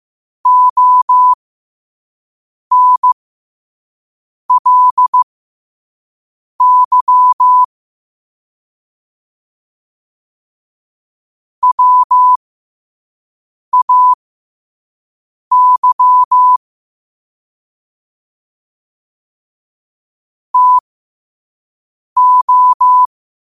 Morse code, heard as a keyed tone, about 15 wpm, spaced as for 5 wpm, 1000 Hz. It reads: ONLY WAY TO
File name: qotdfiles / 5wpm